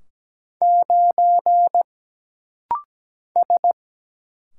Morse code, heard as a keyed tone, about 17 wpm, 700 Hz. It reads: 9 S
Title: Single Letter-Number - Rapid-Fire 17wpm